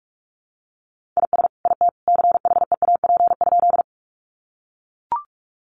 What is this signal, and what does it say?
Morse code, heard as a keyed tone, about 44 wpm, 700 Hz. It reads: IS IT CHEAP?
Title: Sentences from Top 1200 Words 45wpm